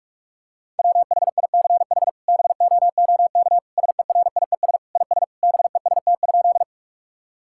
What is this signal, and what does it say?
Morse code, heard as a keyed tone, about 46 wpm, 700 Hz. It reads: WHICH BOOK SERIES IS BEST?
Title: Sentences from Top 1000 Words - Review Part 5 - 45wpm